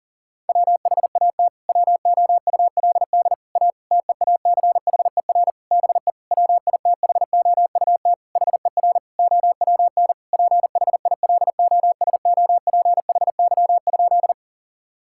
Morse code, heard as a keyed tone, about 40 wpm, 700 Hz. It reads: WHAT WOULD A TEACHER BE WITHOUT HER OWN PHILOSOPHY?
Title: Sentences from Top 1000 Words - Review Part 6 - 40wpm